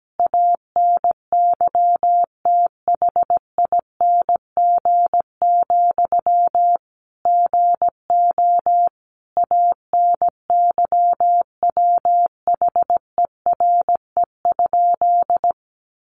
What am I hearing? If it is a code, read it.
Morse code, heard as a keyed tone, about 17 wpm, 700 Hz. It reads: ANYTHING, GO ANYWHERE?
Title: Sentences from Top 1400 Words 17wpm